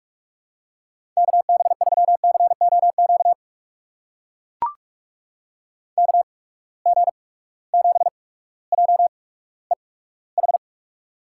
Morse code, heard as a keyed tone, about 44 wpm, spaced as for 19 wpm, 700 Hz. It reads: KB3COQ KG7JEH
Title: USA Call Signs - 6 Characters - 8x Character Spacing - 45wpm